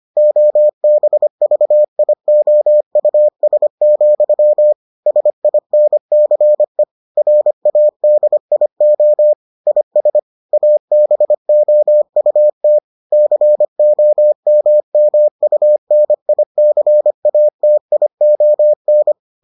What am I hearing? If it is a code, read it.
Morse code, heard as a keyed tone, about 25 wpm, 600 Hz. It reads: OBVIOUS, SINCE RADIO IS ABOUT COMMUNICATION